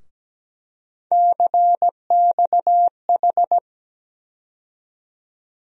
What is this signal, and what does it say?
Morse code, heard as a keyed tone, about 17 wpm, 700 Hz. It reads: CXH
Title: USA Call Sign Postfixes - 3 Letters 17wpm